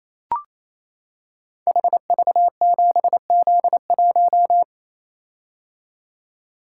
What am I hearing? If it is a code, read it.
Morse code, heard as a keyed tone, about 29 wpm, 700 Hz. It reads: HV7Z1
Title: Code Groups - 5 Letter:Numbers - 28wpm